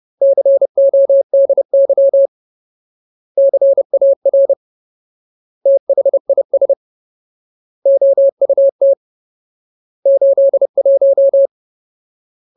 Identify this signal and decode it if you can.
Morse code, heard as a keyed tone, about 30 wpm, 550 Hz. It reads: CODY CAR THIS OUT 81
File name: Ultimate Word List / 30WPM Ultimate